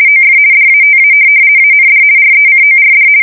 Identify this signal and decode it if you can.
RTTY, about 50 baud. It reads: OF DATA AND RECORD
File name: fsk170-50B/teletype-corp